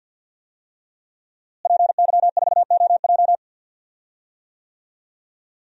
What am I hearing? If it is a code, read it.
Morse code, heard as a keyed tone, about 51 wpm, 700 Hz. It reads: PY4OJ
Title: DXCC-100 Call Signs - 5 Characters - Repeated 3x - 50wpm